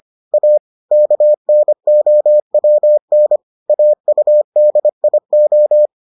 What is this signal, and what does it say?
Morse code, heard as a keyed tone, about 25 wpm, 600 Hz. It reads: A KNOWN AUDIO